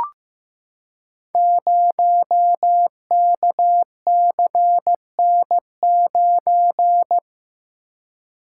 Morse code, heard as a keyed tone, about 15 wpm, 700 Hz. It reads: 0KCN9
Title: Code Groups - 5 Letter:Numbers - 15wpm